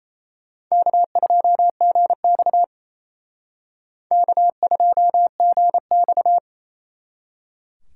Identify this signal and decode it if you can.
Morse code, heard as a keyed tone, about 35 wpm, 700 Hz. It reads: K2GX K2GX
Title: Speed Racing - USA 1x2 Call Signs 28wpm